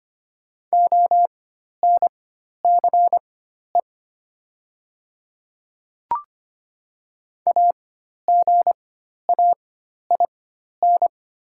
Morse code, heard as a keyed tone, about 25 wpm, spaced as for 12 wpm, 690 Hz. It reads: ONCE AGAIN
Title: 4x Character Spacing - Top 300 Words - 25wpm